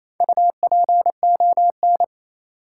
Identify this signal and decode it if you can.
Morse code, heard as a keyed tone, about 28 wpm, 700 Hz. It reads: UPON
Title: Sentences from Top 1000 Words - Review Part 6 - New Word Drill - 28wpm